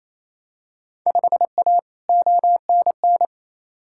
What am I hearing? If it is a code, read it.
Morse code, heard as a keyed tone, about 27 wpm, 700 Hz. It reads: 5A ONN
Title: Field Day Exchanges Encore - 28wpm